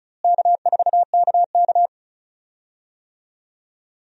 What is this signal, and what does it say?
Morse code, heard as a keyed tone, about 36 wpm, 700 Hz. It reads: K4KK
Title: USA Call Signs - 4 Characters - Repeated 3x - 35wpm